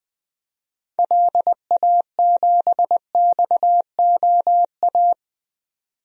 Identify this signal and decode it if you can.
Morse code, heard as a keyed tone, about 21 wpm, 700 Hz. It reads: LA7XOA
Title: DXCC-100 Call Signs - 6 Characters - 20wpm